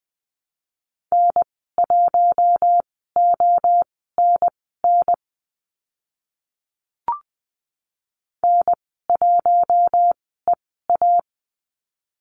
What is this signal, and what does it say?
Morse code, heard as a keyed tone, about 20 wpm, spaced as for 14 wpm, 700 Hz. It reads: N1ONN N1EA
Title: 042 Lesson 8g - Number 1 - Call Signs 20wpm